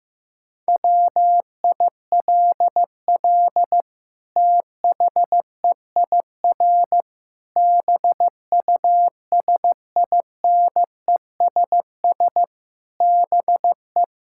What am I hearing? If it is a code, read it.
Morse code, heard as a keyed tone, about 15 wpm, 700 Hz. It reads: WILL THEIR BUSINESS BE
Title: Sentences from Top 1000 Words - Review Part 5 - 15wpm